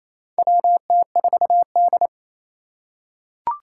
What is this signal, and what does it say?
Morse code, heard as a keyed tone, about 28 wpm, 700 Hz. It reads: WT4D
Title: USA Call Signs - 4 Characters - 4x Character Spacing - 28wpm